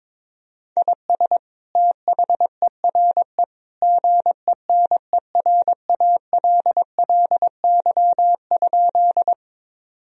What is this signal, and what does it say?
Morse code, heard as a keyed tone, about 22 wpm, 700 Hz. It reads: IS THERE GENERALLY?
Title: Sentences from Top 1000 Words - Review Part 4 - 22wpm